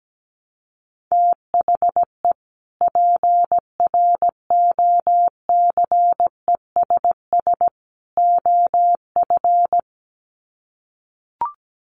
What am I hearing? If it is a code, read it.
Morse code, heard as a keyed tone, about 17 wpm, 700 Hz. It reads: THE PROCESS OF
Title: Sets of 3 Words 17wpm